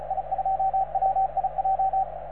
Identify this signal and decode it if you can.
Morse code, heard as a keyed tone, about 38 wpm, 750 Hz, with radio noise. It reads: I2WIJ